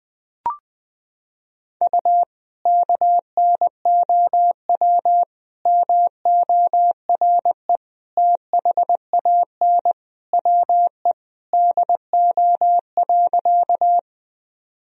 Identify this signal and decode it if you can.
Morse code, heard as a keyed tone, about 20 wpm, 700 Hz. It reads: U KNOW MORE THAN WE DO.
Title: Sentences from Top 100 Words and Abbreviations 20wpm